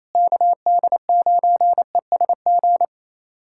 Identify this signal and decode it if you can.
Morse code, heard as a keyed tone, about 28 wpm, 700 Hz. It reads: KD9ESG
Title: USA Callsign DE Callsign 28wpm